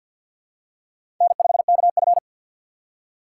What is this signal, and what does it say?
Morse code, heard as a keyed tone, about 53 wpm, 700 Hz. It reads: N5KF
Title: USA Call Signs - 4 Characters - 6x Character Spacing - 50wpm